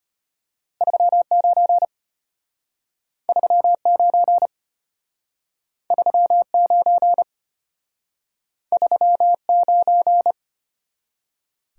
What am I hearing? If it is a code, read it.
Morse code, heard as a keyed tone, about 33 wpm, 700 Hz. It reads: 39 39 39 39